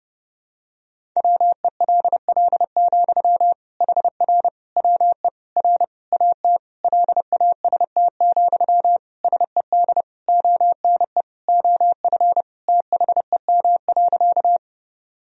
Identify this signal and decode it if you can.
Morse code, heard as a keyed tone, about 32 wpm, 700 Hz. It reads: WELL, HR WE R AT LAST, SED ONE OF THEM.
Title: Sentences from Top 300 Words and Abbreviations 30wpm